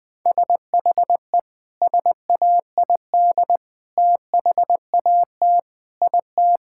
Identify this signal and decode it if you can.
Morse code, heard as a keyed tone, about 20 wpm, 700 Hz. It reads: SHE SAID THAT IT
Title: Sentences with 3-Word Phrases - 20wpm